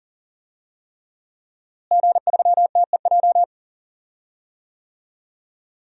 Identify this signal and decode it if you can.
Morse code, heard as a keyed tone, about 39 wpm, 700 Hz. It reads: G3TEJ
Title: DXCC-100 Call Signs - 5 Characters - Repeated 3x - 40wpm